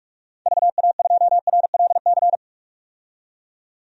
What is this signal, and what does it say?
Morse code, heard as a keyed tone, about 44 wpm, 690 Hz. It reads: VA2RLC
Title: DXCC-100 Call Signs - 6 Characters - 45wpm